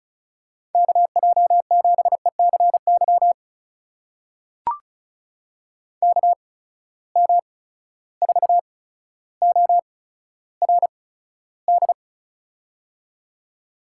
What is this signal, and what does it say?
Morse code, heard as a keyed tone, about 36 wpm, spaced as for 15 wpm, 700 Hz. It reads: KJ7ECY KM4ORD
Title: USA Call Signs - 6 Characters - 8x Character Spacing - 35wpm